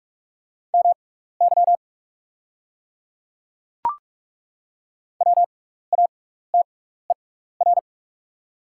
Morse code, heard as a keyed tone, about 48 wpm, spaced as for 16 wpm, 690 Hz. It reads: MY WATER
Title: 6x Character Spacing - Top 100 Words - 45wpm